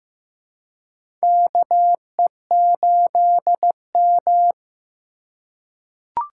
Morse code, heard as a keyed tone, about 15 wpm, 700 Hz. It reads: KE8M
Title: K1USN SST QSO Practice - 15wpm